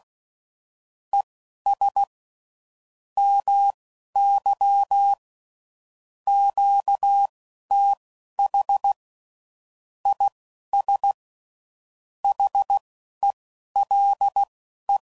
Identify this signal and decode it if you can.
Morse code, heard as a keyed tone, about 16 wpm, spaced as for 11 wpm, 800 Hz. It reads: ES MY QTH IS HELE
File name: CW Code Groups @15WPM / QSOs / WD0BGP Spaced